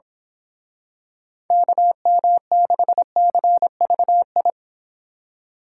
Morse code, heard as a keyed tone, about 27 wpm, 700 Hz. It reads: KM6CVI
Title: Speed Racing - USA 2x3 Call Signs 22wpm